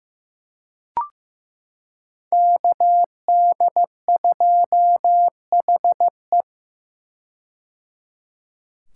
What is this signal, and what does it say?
Morse code, heard as a keyed tone, about 15 wpm, 700 Hz. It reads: KD2HE
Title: K1USN SST QSO Practice - 15wpm